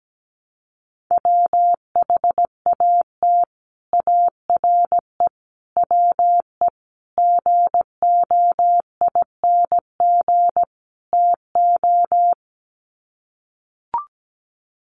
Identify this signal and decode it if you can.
Morse code, heard as a keyed tone, about 17 wpm, 700 Hz. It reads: WHAT ARE WE GOING TO
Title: Sets of 5 Words 17wpm